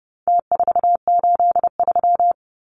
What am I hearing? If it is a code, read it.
Morse code, heard as a keyed tone, about 30 wpm, 700 Hz. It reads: T483
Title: Numbers - Contest Style - 4-Digit 30wpm